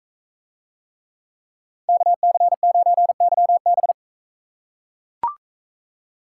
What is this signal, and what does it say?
Morse code, heard as a keyed tone, about 42 wpm, 700 Hz. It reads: KC9YB